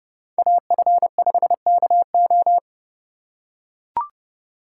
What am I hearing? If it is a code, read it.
Morse code, heard as a keyed tone, about 30 wpm, 700 Hz. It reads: AF5KO